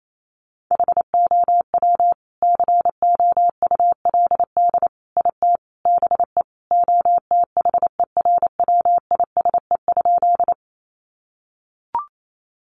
Morse code, heard as a keyed tone, about 28 wpm, 700 Hz. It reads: HOW COULD IT BE OTHERWISE?
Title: Sentences from Top 1200 Words 28wpm